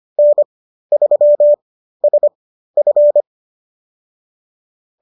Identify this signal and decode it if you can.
Morse code, heard as a keyed tone, about 25 wpm, spaced as for 13 wpm, 590 Hz. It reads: N3SF